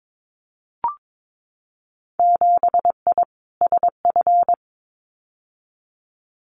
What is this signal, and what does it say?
Morse code, heard as a keyed tone, about 22 wpm, 700 Hz. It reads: 7I SF
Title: Winter Field Day Exchanges Encore 22wpm